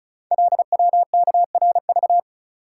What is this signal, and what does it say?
Morse code, heard as a keyed tone, about 36 wpm, 700 Hz. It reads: LWKRV
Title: Code Groups - 5 Letters - 35wpm